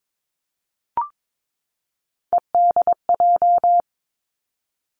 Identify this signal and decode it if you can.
Morse code, heard as a keyed tone, about 22 wpm, 700 Hz. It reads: EDJ
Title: Three Challenging Letter-Number Combinations 22wpm